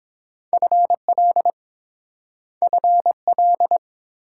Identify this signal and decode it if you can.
Morse code, heard as a keyed tone, about 29 wpm, 700 Hz. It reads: FL FL